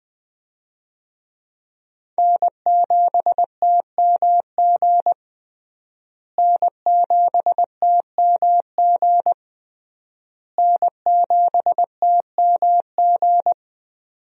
Morse code, heard as a keyed tone, about 19 wpm, 700 Hz. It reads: N7TMG N7TMG N7TMG
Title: USA Call Signs - 5 Characters - Repeated 3x - 20wpm